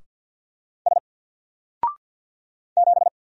Morse code, heard as a keyed tone, about 49 wpm, 700 Hz. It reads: S 7
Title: Single Character (Letter:Number:Punctuation:Prosign) 50wpm